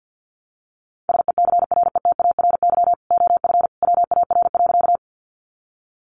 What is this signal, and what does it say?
Morse code, heard as a keyed tone, about 50 wpm, 700 Hz. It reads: SECRETARY OF WAR.
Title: Sentences from Top 1300 Words 50wpm